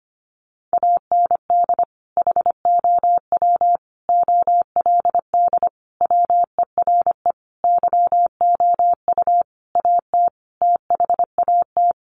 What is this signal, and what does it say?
Morse code, heard as a keyed tone, about 25 wpm, 700 Hz. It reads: AND HOW OLD WERE YOU AT THAT
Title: Sentences from Top 200 Words 25wpm